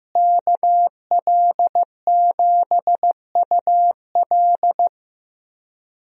Morse code, heard as a keyed tone, about 15 wpm, 700 Hz. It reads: KL7UL
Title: USA Call Signs - 5 Characters - Repeated 3x - 15wpm